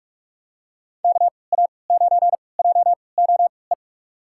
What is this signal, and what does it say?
Morse code, heard as a keyed tone, about 46 wpm, spaced as for 26 wpm, 700 Hz. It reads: KA9JOE
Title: USA Call Signs - 3x Character Spacing - 45wpm